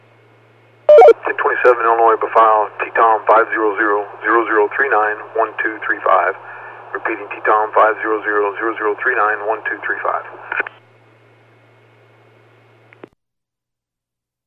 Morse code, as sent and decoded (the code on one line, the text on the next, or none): none